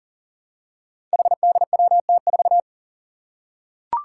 Morse code, heard as a keyed tone, about 40 wpm, 700 Hz. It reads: HDWT4
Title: Five Challenging Letter-Number Combinations 40wpm